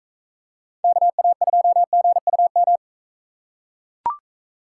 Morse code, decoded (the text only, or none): KA2GUM